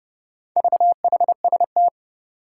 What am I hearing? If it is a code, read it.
Morse code, heard as a keyed tone, about 29 wpm, 700 Hz. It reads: VHST
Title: Four Challenging Letter-Number Combinations 30wpm